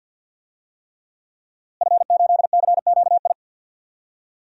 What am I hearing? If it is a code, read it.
Morse code, heard as a keyed tone, about 53 wpm, 700 Hz. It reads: F8CQI
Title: DXCC-100 Call Signs - 5 Characters - Repeated 3x - 50wpm